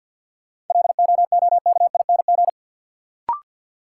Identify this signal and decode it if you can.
Morse code, heard as a keyed tone, about 49 wpm, 700 Hz. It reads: LOOKING
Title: Sentences from Top 1000 Words - Review Part 6 - New Word Drill - 50wpm